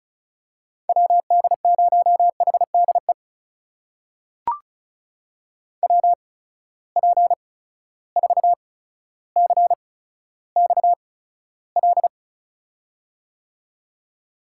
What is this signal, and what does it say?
Morse code, heard as a keyed tone, about 36 wpm, spaced as for 15 wpm, 700 Hz. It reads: WD0HDE WP4CXL
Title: USA Call Signs - 6 Characters - 8x Character Spacing - 35wpm